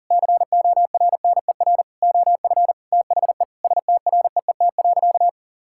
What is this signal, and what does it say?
Morse code, heard as a keyed tone, about 40 wpm, 700 Hz. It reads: CORNER OF THE STREET.
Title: Sentences from Top 1600 Words 40wpm